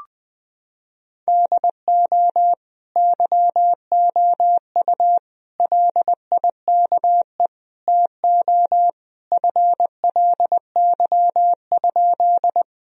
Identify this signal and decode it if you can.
Morse code, heard as a keyed tone, about 20 wpm, 700 Hz. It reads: DO YOU LIKE TO FLY?
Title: Sentences from Top 1000 Words - Review Part 2 - 20wpm